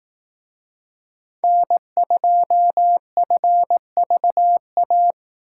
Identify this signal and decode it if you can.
Morse code, heard as a keyed tone, about 18 wpm, 700 Hz. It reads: N2FVA
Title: Speed Racing - USA 1x3 Call Signs 15wpm